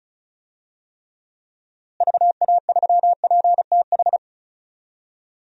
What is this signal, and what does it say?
Morse code, heard as a keyed tone, about 34 wpm, 700 Hz. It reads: VA3PTH
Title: DXCC-100 Call Signs - 6 Characters - Repeated 2x - 35wpm